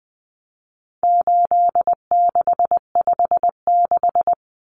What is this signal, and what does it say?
Morse code, heard as a keyed tone, about 20 wpm, 700 Hz. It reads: 8656
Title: Common Cut Numbers - 4-Digit 20wpm